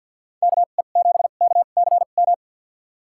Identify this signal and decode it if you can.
Morse code, heard as a keyed tone, about 48 wpm, spaced as for 35 wpm, 700 Hz. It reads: KE7KCM
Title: USA Call Signs - 2x Character Spacing - 50wpm